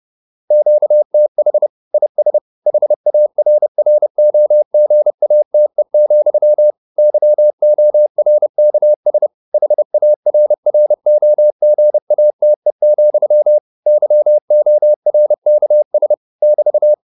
Morse code, heard as a keyed tone, about 30 wpm, 600 Hz. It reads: QTH IS HARROGATE, YORKS HARROGATE, YORKS <BT>
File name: QSO 404 30